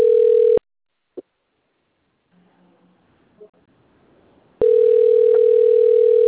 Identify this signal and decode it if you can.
Ringback tone (North America), 440 + 480 Hz.